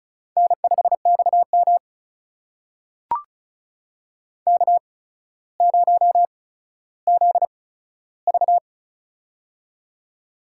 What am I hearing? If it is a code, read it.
Morse code, heard as a keyed tone, about 36 wpm, spaced as for 15 wpm, 700 Hz. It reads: N5XM K0ZV
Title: USA Call Signs - 4 Characters - 8x Character Spacing - 35wpm